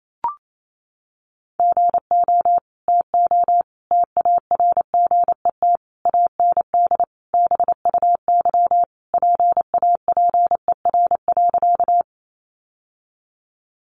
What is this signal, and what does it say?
Morse code, heard as a keyed tone, about 28 wpm, 700 Hz. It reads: GO TO TARGET AND BUY PAPER.
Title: Sentences from Top 1200 Words 28wpm